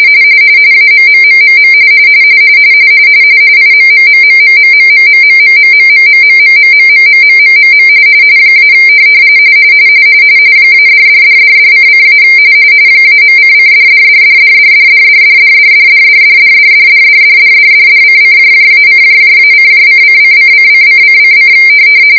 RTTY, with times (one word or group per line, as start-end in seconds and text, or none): 0.7-1.7 AMMMMM
3.6-7.9 IMMMMMMMMMMMMMMMMMMMMMMMMM
8.6-8.7 -
10.7-12.2 -$$$$$::-
13.0-21.5 -.::$$$$$$$$$$$$$$$$$$$$$$$$$$::-:-:::::::::::